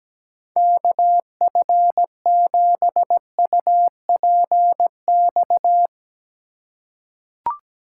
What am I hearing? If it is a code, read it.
Morse code, heard as a keyed tone, about 17 wpm, 700 Hz. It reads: KF7UPX
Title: USA Call Signs - 6 Characters - 2x Character Spacing - 17wpm